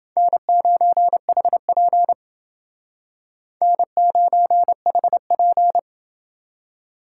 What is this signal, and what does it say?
Morse code, heard as a keyed tone, about 29 wpm, 700 Hz. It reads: N9HP N9HP